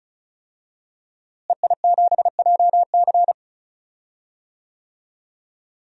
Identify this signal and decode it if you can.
Morse code, heard as a keyed tone, about 36 wpm, 700 Hz. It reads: EI7JC
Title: DXCC-100 Call Signs - 5 Characters - Repeated 2x - 35wpm